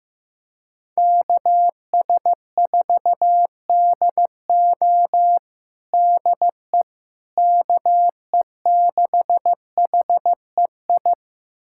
Morse code, heard as a keyed tone, about 15 wpm, 700 Hz. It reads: KS4DO DE KE6HEI
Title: USA Callsign DE Callsign 15wpm